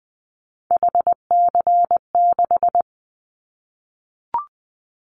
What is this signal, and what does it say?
Morse code, heard as a keyed tone, about 20 wpm, 700 Hz. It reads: HC6
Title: Callsign Prefixes for Top 100 DXCC - 20wpm